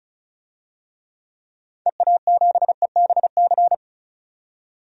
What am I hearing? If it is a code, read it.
Morse code, heard as a keyed tone, about 36 wpm, 690 Hz. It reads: EA7EBC
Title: DXCC-100 Call Signs - 6 Characters - 35wpm